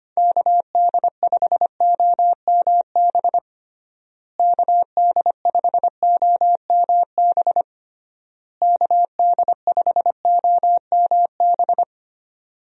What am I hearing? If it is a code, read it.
Morse code, heard as a keyed tone, about 25 wpm, 700 Hz. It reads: KD5OMB KD5OMB KD5OMB
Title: USA Call Signs - 6 Characters - Repeated 3x - 25wpm